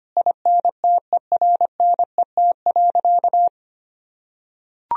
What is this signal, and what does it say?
Morse code, heard as a keyed tone, about 25 wpm, 700 Hz. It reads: INTERNET.
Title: Sentences from Top 1000 Words - Review Part 4 - 25wpm